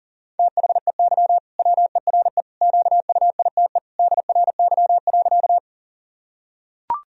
Morse code, heard as a keyed tone, about 37 wpm, 700 Hz. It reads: THEY WERE QUITE DRY.
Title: Sentences from Top 1000 Words - Review Part 3 - 40wpm